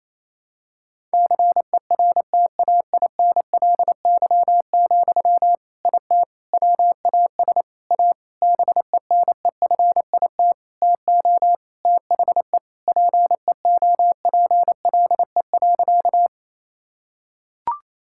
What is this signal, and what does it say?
Morse code, heard as a keyed tone, about 28 wpm, 700 Hz. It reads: CERTAINLY, IT WAS A BENEFIT TO THE PEOPLE.
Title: Sentences from Top 1000 Words - Review Part 6 - 28wpm